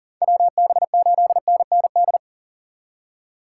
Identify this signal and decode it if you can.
Morse code, heard as a keyed tone, about 52 wpm, 700 Hz. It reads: WB8NND